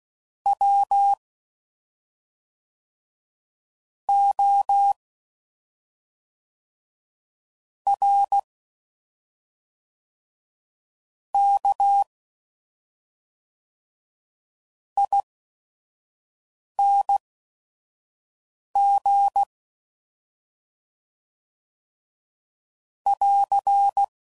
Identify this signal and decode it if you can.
Morse code, heard as a keyed tone, about 16 wpm, spaced as for 5 wpm, 800 Hz. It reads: W O R K ING <AR>
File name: Suffixes / Special ING's / SlowToFaster ING's